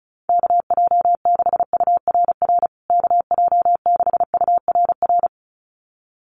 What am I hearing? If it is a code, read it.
Morse code, heard as a keyed tone, about 34 wpm, 700 Hz. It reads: KJ6URR KJ6URR
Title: USA Call Signs Repeated 2x 35wpm